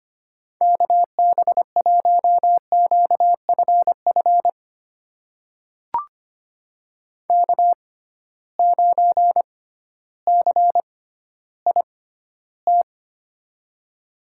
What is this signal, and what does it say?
Morse code, heard as a keyed tone, about 25 wpm, spaced as for 14 wpm, 700 Hz. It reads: KB1QFF K9CIT
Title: USA Call Signs - 6x Character Spacing - 25wpm